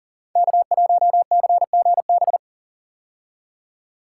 Morse code, heard as a keyed tone, about 42 wpm, 700 Hz. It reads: K1CGB
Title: Code Groups - 5 Letter:Numbers - 40wpm